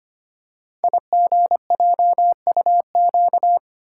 Morse code, heard as a keyed tone, about 25 wpm, 700 Hz. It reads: IGJUQ
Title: Code Groups - 5 Letters - 25wpm